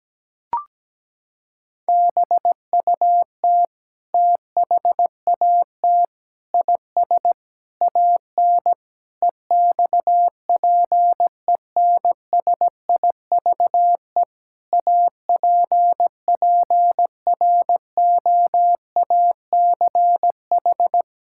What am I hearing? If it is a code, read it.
Morse code, heard as a keyed tone, about 17 wpm, 700 Hz. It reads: BUT THAT IS AN EXPENSIVE APPROACH
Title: Sentences from Top 1000 Words - Review Part 6 - 17wpm